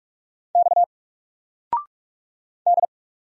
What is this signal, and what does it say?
Morse code, heard as a keyed tone, about 46 wpm, 700 Hz. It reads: X D
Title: Single Letter-Number 45wpm